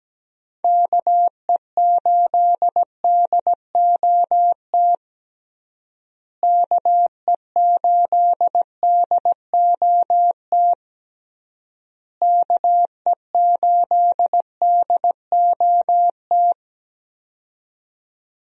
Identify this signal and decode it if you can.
Morse code, heard as a keyed tone, about 17 wpm, 700 Hz. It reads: KE8DOT KE8DOT KE8DOT
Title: USA Call Signs - 6 Characters - Repeated 3x - 17wpm